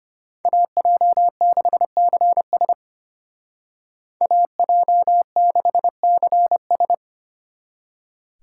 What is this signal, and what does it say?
Morse code, heard as a keyed tone, about 33 wpm, 700 Hz. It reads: AJ6CS AJ6CS